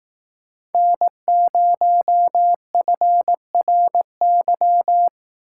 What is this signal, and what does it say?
Morse code, heard as a keyed tone, about 18 wpm, 700 Hz. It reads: N0FRY
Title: Speed Racing - USA 1x3 Call Signs 15wpm